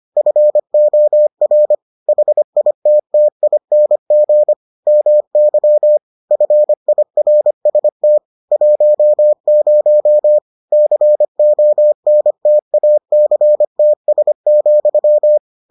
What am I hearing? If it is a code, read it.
Morse code, heard as a keyed tone, about 25 wpm, 600 Hz. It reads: FOR HITTING MY FIRST 10 CONTACTS,